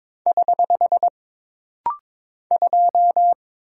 Morse code, heard as a keyed tone, about 22 wpm, 700 Hz. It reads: <HH> 2
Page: This rapid-fire timing is geared to push your ICR (Instant Character Recognition) skills.